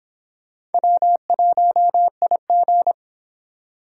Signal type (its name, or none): Morse code